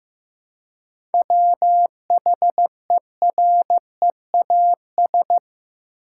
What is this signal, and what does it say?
Morse code, heard as a keyed tone, about 15 wpm, 700 Hz. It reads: WHEREAS
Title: Seven Letter Words - 15wpm